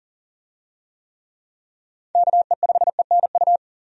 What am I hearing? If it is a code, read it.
Morse code, heard as a keyed tone, about 40 wpm, 700 Hz. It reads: KE5ENU
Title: USA Call Signs - 6 Characters - 40wpm